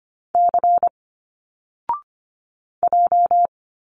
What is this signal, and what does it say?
Morse code, heard as a keyed tone, about 25 wpm, 700 Hz. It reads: C J
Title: Single Letters 25wpm